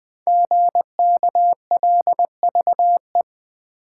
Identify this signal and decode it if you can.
Morse code, heard as a keyed tone, about 20 wpm, 700 Hz. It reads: GKLVE
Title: Code Groups - 5 Letter:Numbers - 20wpm